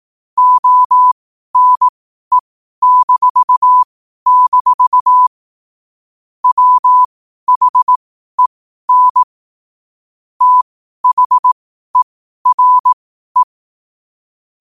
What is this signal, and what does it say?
Morse code, heard as a keyed tone, about 18 wpm, spaced as for 12 wpm, 1000 Hz. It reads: ONE-- WHEN THERE
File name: qotdfiles / 12wpm